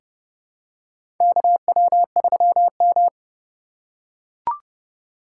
Morse code, heard as a keyed tone, about 30 wpm, 700 Hz. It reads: KW3M
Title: USA Call Signs - 2x Character Spacing - 30wpm